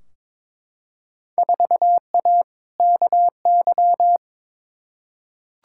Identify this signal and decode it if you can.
Morse code, heard as a keyed tone, about 22 wpm, 700 Hz. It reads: 4A KY
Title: Field Day Exchanges Encore - 22wpm